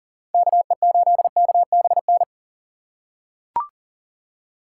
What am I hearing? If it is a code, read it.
Morse code, heard as a keyed tone, about 38 wpm, 700 Hz. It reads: KE8KBN
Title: USA Call Signs - 3x Character Spacing - 40wpm